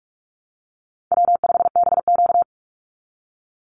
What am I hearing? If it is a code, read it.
Morse code, heard as a keyed tone, about 44 wpm, 700 Hz. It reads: W5BQ
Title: Easy Call Signs Repeated 2x - 45wpm